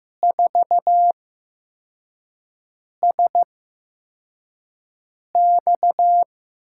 Morse code, heard as a keyed tone, about 15 wpm, spaced as for 4 wpm, 700 Hz. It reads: 4SX
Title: USA Call Signs - 6 Characters - 8x Character Spacing - 15wpm